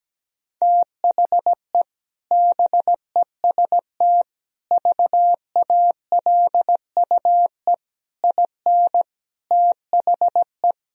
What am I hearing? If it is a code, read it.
Morse code, heard as a keyed tone, about 17 wpm, 700 Hz. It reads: THE BEST VALUE IN THE
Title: Sentences from Top 1000 Words - Review Part 3 - 17wpm